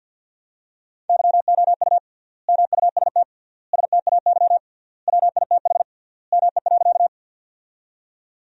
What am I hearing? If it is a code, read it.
Morse code, heard as a keyed tone, about 50 wpm, 700 Hz. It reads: YOU MUST STAY WITH ME.
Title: 3x Word-Spacing - Sentences from Top 500 Words - 50wpm